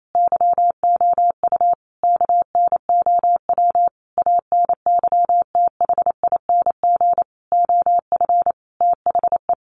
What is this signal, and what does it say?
Morse code, heard as a keyed tone, about 28 wpm, 700 Hz. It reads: YOU KNOW ANYTHING OF THE
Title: Sentences from Top 1000 Words 28wpm